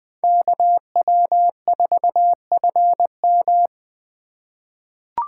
Morse code, heard as a keyed tone, about 20 wpm, 700 Hz. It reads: KW4FM
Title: 2023 POTA Hunter Call Signs - 20wpm